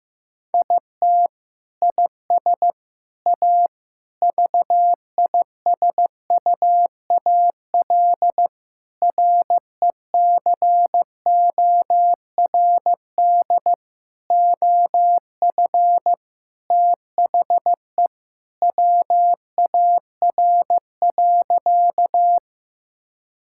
Morse code, heard as a keyed tone, about 14 wpm, 700 Hz. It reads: IT IS A VISUAL RECORD OF THE WAR.
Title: Sentences from Top 1000 Words - Review Part 6 - 15wpm